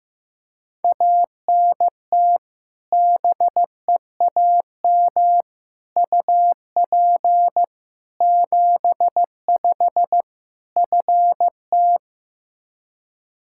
Morse code, heard as a keyed tone, about 15 wpm, 700 Hz. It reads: ANT BEAM UP 75 FT
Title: Practice QSO - Exchange 2 - Info - 15wpm